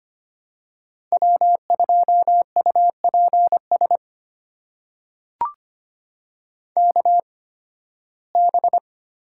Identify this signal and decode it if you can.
Morse code, heard as a keyed tone, about 25 wpm, spaced as for 16 wpm, 700 Hz. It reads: W2UPS KB
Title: USA Call Signs - 5 Characters - 8x Character Spacing - 25wpm